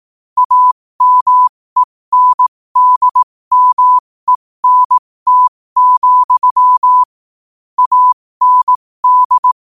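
Morse code, heard as a keyed tone, about 18 wpm, spaced as for 15 wpm, 1000 Hz. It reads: AMENDMENT, AND
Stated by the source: Morse code quote of the day sent at 15 words per minute.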